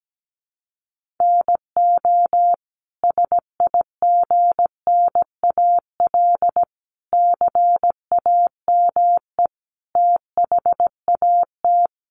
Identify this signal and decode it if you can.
Morse code, heard as a keyed tone, about 17 wpm, 700 Hz. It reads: NO SIGNAL CAME THAT
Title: Sentences from Top 1400 Words 17wpm